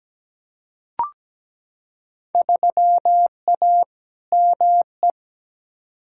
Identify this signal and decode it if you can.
Morse code, heard as a keyed tone, about 17 wpm, 700 Hz. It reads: 3A ME
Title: Field Day Exchanges - 17wpm